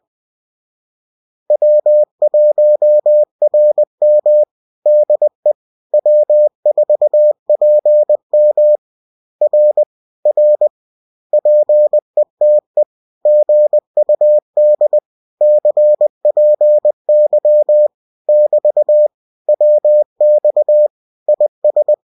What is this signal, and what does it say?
Morse code, heard as a keyed tone, about 20 wpm, 600 Hz. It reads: W1RM DE W4PM R R PETE GUD CPY <BT> WX IS